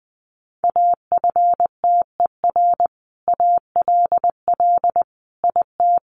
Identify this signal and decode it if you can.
Morse code, heard as a keyed tone, about 20 wpm, 700 Hz. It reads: AFTER ALL IT
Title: Sentences from Top 1200 Words 20wpm